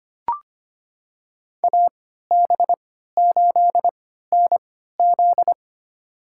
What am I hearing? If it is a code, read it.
Morse code, heard as a keyed tone, about 25 wpm, spaced as for 14 wpm, 700 Hz. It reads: AB8NZ
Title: USA Call Signs - 5 Characters - 3x Character Spacing - 25wpm